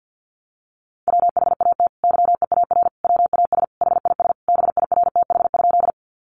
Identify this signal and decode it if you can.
Morse code, heard as a keyed tone, about 52 wpm, 700 Hz. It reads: WHAT YEAR WAS HIS BIRTH?
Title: Sentences from Top 800 Words 50wpm